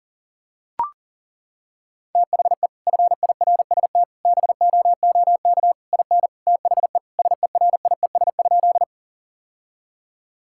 Morse code, heard as a keyed tone, about 40 wpm, 700 Hz. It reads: THE FIRST BOOK IN THE SERIES?
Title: Sentences from Top 1000 Words - Review Part 5 - 40wpm